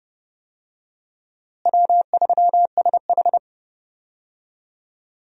Morse code, heard as a keyed tone, about 30 wpm, 700 Hz. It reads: W3SH
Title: USA Call Signs - 4 Characters - Repeated 3x - 30wpm